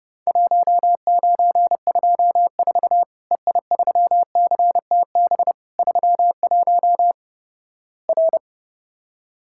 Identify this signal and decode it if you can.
Morse code, heard as a keyed tone, about 30 wpm, 700 Hz. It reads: 1924 EI3CTB 31 R
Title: Worked All Europe DX Contest - QTC Exchange - 30wpm